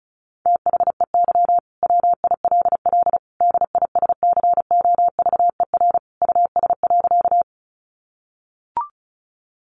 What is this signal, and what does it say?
Morse code, heard as a keyed tone, about 34 wpm, 700 Hz. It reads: THEY WILL DISCOVER US.
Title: Sentences from Top 1100 Words 35wpm